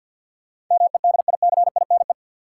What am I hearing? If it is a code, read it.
Morse code, heard as a keyed tone, about 48 wpm, 700 Hz. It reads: MEDICINE
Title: Sentences from Top 900 Words - New Word Drill - 50wpm